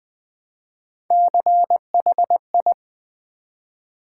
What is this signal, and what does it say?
Morse code, heard as a keyed tone, about 20 wpm, 700 Hz. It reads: CHI
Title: Build A Word - 1 Letter At A Time - 5-Letter Words - 20wpm